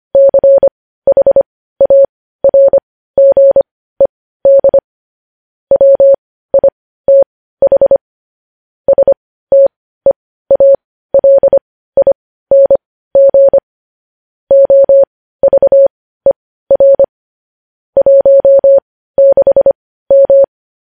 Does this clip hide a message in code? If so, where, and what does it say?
Yes: Morse code from 0.1 to 20.4 s, heard as a keyed tone, about 25 wpm, spaced as for 15 wpm, 550 Hz. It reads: CHARGED WITH STEALING OVER 16M